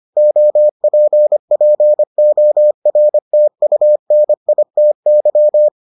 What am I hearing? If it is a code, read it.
Morse code, heard as a keyed tone, about 25 wpm, 600 Hz. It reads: OPPORTUNITY